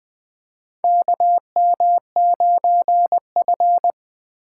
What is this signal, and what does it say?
Morse code, heard as a keyed tone, about 20 wpm, 700 Hz. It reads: KM9F
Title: USA Call Signs - 4 Characters - 3x Character Spacing - 20wpm